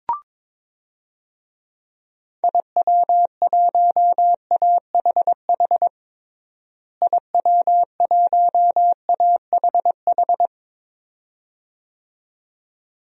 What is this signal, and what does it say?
Morse code, heard as a keyed tone, about 22 wpm, 700 Hz. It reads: IW1AHH IW1AHH
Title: DXCC-100 Call Signs - 6 Characters - Repeated 2x - 22wpm